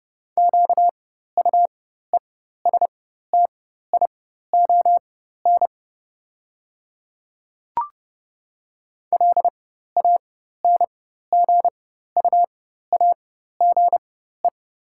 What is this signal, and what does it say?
Morse code, heard as a keyed tone, about 30 wpm, spaced as for 14 wpm, 700 Hz. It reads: QUESTION LANGUAGE
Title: 4x Character Spacing - Top 500 Words - 30wpm